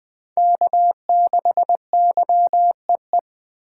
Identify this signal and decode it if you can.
Morse code, heard as a keyed tone, about 20 wpm, 700 Hz. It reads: K6YEE
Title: USA Call Signs - 5 Characters - 4x Character Spacing - 20wpm